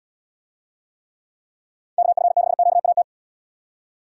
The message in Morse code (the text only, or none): KF6QAT